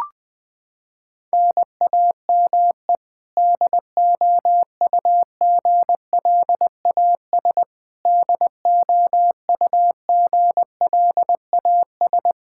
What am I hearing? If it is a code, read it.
Morse code, heard as a keyed tone, about 20 wpm, 700 Hz. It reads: NAME DOUGLAS DOUGLAS
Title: Practice QSO - Exchange 1 - Beginning - 20wpm